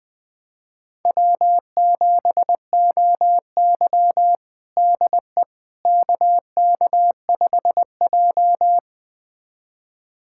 Morse code, heard as a keyed tone, about 20 wpm, 700 Hz. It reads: W7OY DE KK5J
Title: USA Callsign DE Callsign - 4 Characters - 20wpm